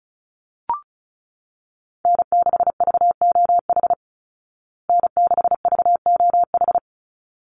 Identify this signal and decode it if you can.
Morse code, heard as a keyed tone, about 34 wpm, 700 Hz. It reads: N6VOH N6VOH
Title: Call Signs Repeated 3x - 35wpm